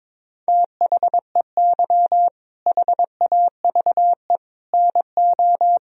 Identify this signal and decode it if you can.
Morse code, heard as a keyed tone, about 22 wpm, 700 Hz. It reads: THEY HAVE NO
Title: Sentences from Top 1000 Words - Review Part 1 - 22wpm